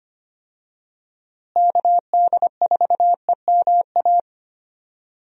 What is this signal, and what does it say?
Morse code, heard as a keyed tone, about 25 wpm, 700 Hz. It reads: KD4EMA
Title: USA Call Signs - 6 Characters - Repeated 2x - 25wpm